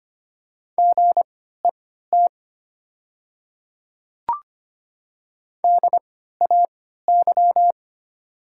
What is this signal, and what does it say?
Morse code, heard as a keyed tone, about 25 wpm, spaced as for 14 wpm, 700 Hz. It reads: GET DAY
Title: 3x Character Spacing - Top 200 Words - 25wpm